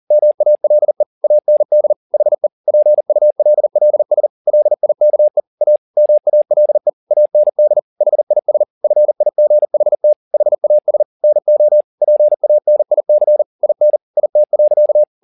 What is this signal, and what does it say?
Morse code, heard as a keyed tone, about 40 wpm, 600 Hz. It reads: MALE AND HE PULLS LIKE A MALE AND HIS FIGHT HAS NO PANIC IN IT.